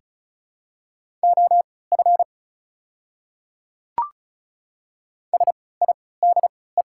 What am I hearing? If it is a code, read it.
Morse code, heard as a keyed tone, about 34 wpm, spaced as for 20 wpm, 700 Hz. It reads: OF SIDE
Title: 3x Character Spacing - Top 100 Words - 35wpm